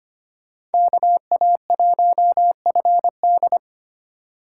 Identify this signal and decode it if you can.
Morse code, heard as a keyed tone, about 32 wpm, 700 Hz. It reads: KA1FD